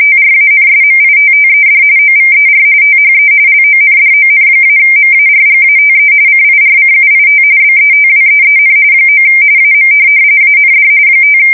RTTY, about 45.45 baud. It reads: TRANSMITTED OVER THE LONDON-MARGATE CIRCUIT BETWEEN 9 AM AND 5
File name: fsk170 / morkrum